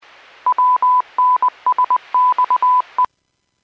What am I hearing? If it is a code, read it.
Morse code, heard as a keyed tone, about 20 wpm, 1000 Hz. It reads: WNSXE